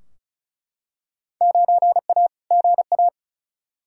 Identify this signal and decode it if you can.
Morse code, heard as a keyed tone, about 36 wpm, 700 Hz. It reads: 9A GA
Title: Field Day Exchanges Encore - 35wpm